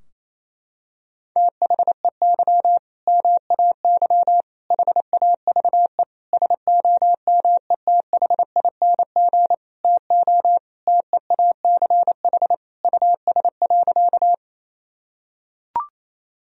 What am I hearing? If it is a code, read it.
Morse code, heard as a keyed tone, about 28 wpm, 700 Hz. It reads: THEY MAY HAVE SOMETHING TO TEACH US.
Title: Sentences from Top 1000 Words - Review Part 2 - 28wpm